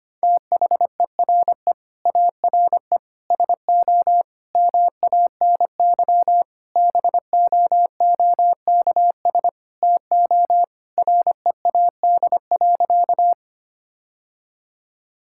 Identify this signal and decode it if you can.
Morse code, heard as a keyed tone, about 25 wpm, 700 Hz. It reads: THERE ARE SO MANY BOOKS TO READ.
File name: practice / 25wpm / Sentences with 4-Word Phrases - 25wpm